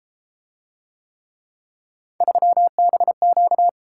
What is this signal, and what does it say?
Morse code, heard as a keyed tone, about 31 wpm, 700 Hz. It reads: 3BQ